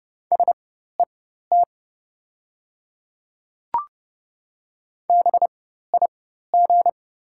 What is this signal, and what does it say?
Morse code, heard as a keyed tone, about 30 wpm, spaced as for 14 wpm, 690 Hz. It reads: SET BIG
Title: 4x Character Spacing - Top 200 Words - 30wpm